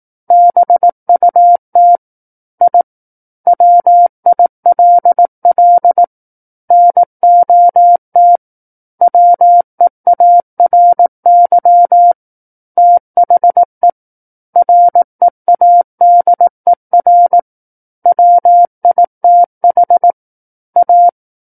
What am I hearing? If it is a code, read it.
Morse code, heard as a keyed tone, about 18 wpm, 700 Hz. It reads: BUT I WILL NOT WEARY THE READER WITH A
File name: 18 WPM